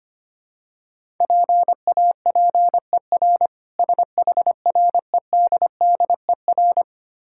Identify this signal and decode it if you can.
Morse code, heard as a keyed tone, about 25 wpm, 700 Hz. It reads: PAPER SHREDDER
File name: practice / 25wpm / Difficult Words - Speed-Racing with Hints - 25wpm